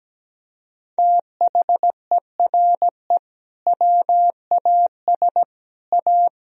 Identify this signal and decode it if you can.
Morse code, heard as a keyed tone, about 17 wpm, 700 Hz. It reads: THERE WAS A
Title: Sentences from Top 1000 Words - Review Part 3 - 17wpm